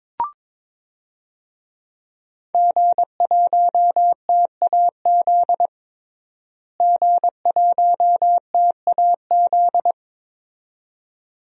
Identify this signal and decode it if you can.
Morse code, heard as a keyed tone, about 22 wpm, 700 Hz. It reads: G1TAZ G1TAZ
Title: DXCC-100 Call Signs - 5 Characters - Repeated 2x - 22wpm